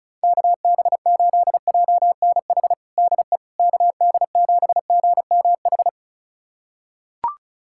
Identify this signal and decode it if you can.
Morse code, heard as a keyed tone, about 36 wpm, 700 Hz. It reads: KB8JNH DE KD7GMH
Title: USA Callsign DE Callsign - 6 Characters - 35wpm